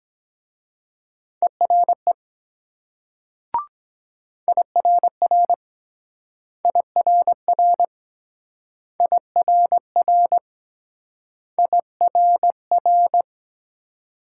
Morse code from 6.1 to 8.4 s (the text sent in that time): IRR